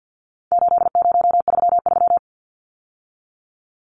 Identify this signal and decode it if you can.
Morse code, heard as a keyed tone, about 50 wpm, 700 Hz. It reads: 8033
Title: Numbers - 4-Digit 50wpm